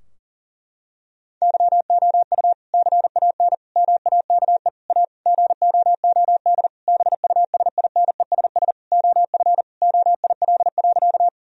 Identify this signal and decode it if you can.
Morse code, heard as a keyed tone, about 40 wpm, 700 Hz. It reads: YOU CAN MAKE A GOOD BUSINESS OF OIL.
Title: Sentences from Top 1000 Words - Review Part 4 - 40wpm